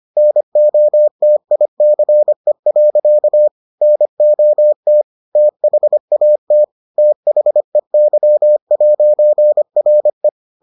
Morse code, heard as a keyed tone, about 25 wpm, 600 Hz. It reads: NOTICE. NOT THAT THEY'RE